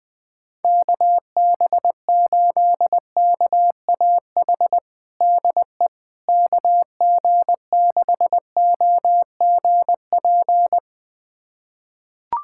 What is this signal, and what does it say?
Morse code, heard as a keyed tone, about 20 wpm, 700 Hz. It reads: KB8KAH DE KG6OGP
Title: USA Callsign DE Callsign - 6 Characters - 20wpm